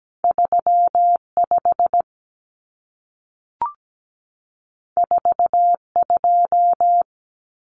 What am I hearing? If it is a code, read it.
Morse code, heard as a keyed tone, about 17 wpm, 700 Hz. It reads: 3 5 4 2
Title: Common Cut Numbers - 2-Digit 17wpm